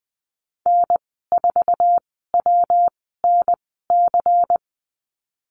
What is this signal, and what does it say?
Morse code, heard as a keyed tone, about 19 wpm, spaced as for 14 wpm, 700 Hz. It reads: N4WNC
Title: 140 Lesson 23g - Letter f - Call Signs 20wpm